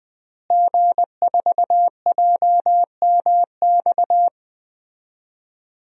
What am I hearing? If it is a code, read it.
Morse code, heard as a keyed tone, about 20 wpm, 700 Hz. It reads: G4JMX
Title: DXCC-100 Call Signs - 5 Characters - Repeated 3x - 20wpm